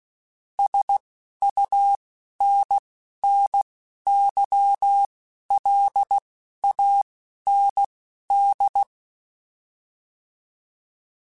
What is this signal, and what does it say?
Morse code, heard as a keyed tone, about 16 wpm, spaced as for 11 wpm, 800 Hz. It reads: SUNNYLAND